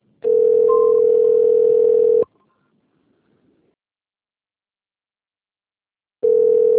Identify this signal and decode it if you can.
Ringback tone (North America), 440 + 480 Hz.